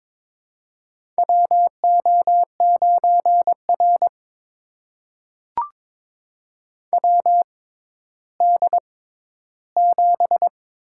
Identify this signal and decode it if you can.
Morse code, heard as a keyed tone, about 22 wpm, spaced as for 12 wpm, 700 Hz. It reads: WO9R WD7
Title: USA Call Signs - 4 Characters - 6x Character Spacing - 22wpm